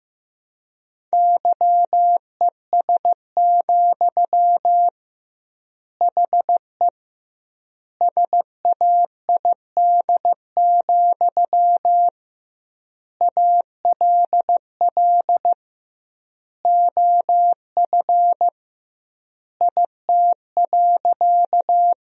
Morse code, heard as a keyed tone, about 16 wpm, 700 Hz. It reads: YES, HE SAID, ALL OF IT.
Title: 2x Word-Spacing - Sentences from Top 500 Words - 15wpm